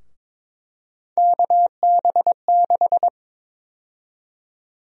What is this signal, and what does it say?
Morse code, heard as a keyed tone, about 22 wpm, 700 Hz. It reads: KB6
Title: USA Call Sign Prefixes - Repeated 2x - 22wpm